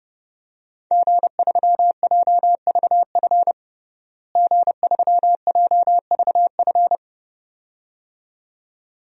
Morse code, heard as a keyed tone, about 29 wpm, 700 Hz. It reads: G3JVF G3JVF
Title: DXCC-100 Call Signs - 5 Characters - Repeated 2x - 30wpm